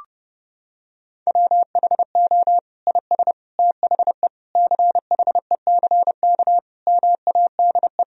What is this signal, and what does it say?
Morse code, heard as a keyed tone, about 30 wpm, 700 Hz. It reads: WHO IS THE CHECK MADE
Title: Sentences from Top 1000 Words - Review Part 3 - 30wpm